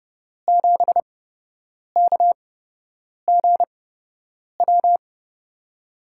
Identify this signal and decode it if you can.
Morse code, heard as a keyed tone, about 30 wpm, spaced as for 8 wpm, 700 Hz. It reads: 7KGW
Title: USA Call Signs - 6 Characters - 8x Character Spacing - 30wpm